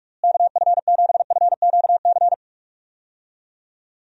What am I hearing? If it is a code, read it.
Morse code, heard as a keyed tone, about 46 wpm, 700 Hz. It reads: KF7FQC
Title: USA Call Signs - 6 Characters - Repeated 2x - 45wpm